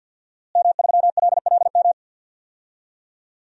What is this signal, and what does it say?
Morse code, heard as a keyed tone, about 47 wpm, 700 Hz. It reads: M3LLM
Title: DXCC-100 Call Signs - 5 Characters - Repeated 2x - 50wpm